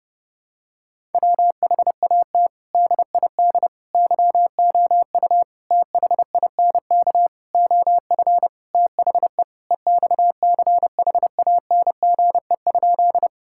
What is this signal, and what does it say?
Morse code, heard as a keyed tone, about 30 wpm, 700 Hz. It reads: WHAT DID YOU THINK OF THE EXCHANGE?
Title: Sentences from Top 1000 Words - Review Part 6 - 30wpm